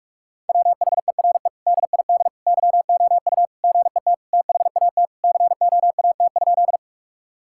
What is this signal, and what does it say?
Morse code, heard as a keyed tone, about 46 wpm, 700 Hz. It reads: WHERE DID YOU GET THAT COAT?
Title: Sentences from Top 1600 Words 45wpm